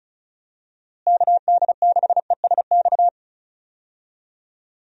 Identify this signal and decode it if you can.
Morse code, heard as a keyed tone, about 36 wpm, 700 Hz. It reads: KD6ESX
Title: USA Call Signs - 6 Characters - Repeated 2x - 35wpm